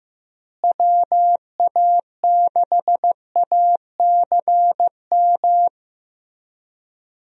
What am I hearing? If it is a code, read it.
Morse code, heard as a keyed tone, about 15 wpm, 700 Hz. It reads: WA6ACM
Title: USA Call Signs - 6 Characters - Repeated 3x - 15wpm